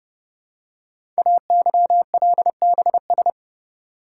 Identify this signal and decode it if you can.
Morse code, heard as a keyed tone, about 28 wpm, 700 Hz. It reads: AYLBS
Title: Code Groups - 5 Letter:Numbers - 30wpm